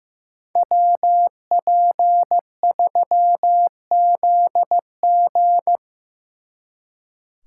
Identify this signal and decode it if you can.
Morse code, heard as a keyed tone, about 17 wpm, 700 Hz. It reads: WP3ZG
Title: Speed Racing - USA Call Signs 15wpm